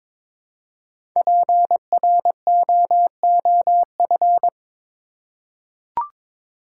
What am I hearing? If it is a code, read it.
Morse code, heard as a keyed tone, about 22 wpm, 690 Hz. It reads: PROOF
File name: 5 Letter Words - First 3 Letters Given - 22wpm